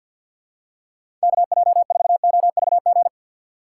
Morse code, heard as a keyed tone, about 53 wpm, 700 Hz. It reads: KJ4OVG
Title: USA Call Signs - 6 Characters - 3x Character Spacing - 50wpm